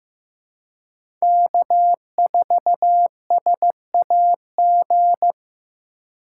Morse code, heard as a keyed tone, about 15 wpm, 700 Hz. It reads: K4SAG
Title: USA Call Signs - 5 Characters - 3x Character Spacing - 15wpm